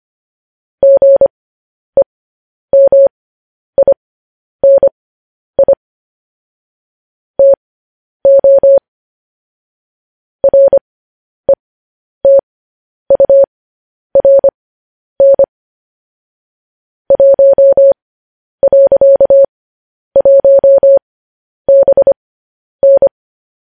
Morse code, heard as a keyed tone, about 25 wpm, spaced as for 10 wpm, 550 Hz. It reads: GEMINI TO RETURN 1.1BN